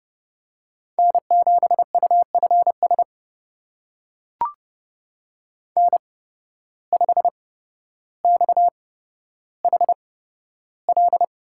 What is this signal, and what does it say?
Morse code, heard as a keyed tone, about 30 wpm, spaced as for 13 wpm, 700 Hz. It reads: N7UFS N5XHL
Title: USA Call Signs - 5 Characters - 8x Character Spacing - 30wpm